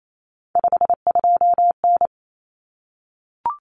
Morse code, heard as a keyed tone, about 28 wpm, 700 Hz. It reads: 52N
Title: Common Cut Numbers - 3-Digit 28wpm